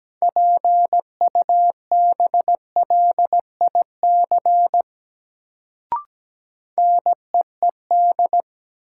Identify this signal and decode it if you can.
Morse code, heard as a keyed tone, about 17 wpm, 700 Hz. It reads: PUBLIC NEED
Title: Sets of 2 Words - New Word Drill - 17wpm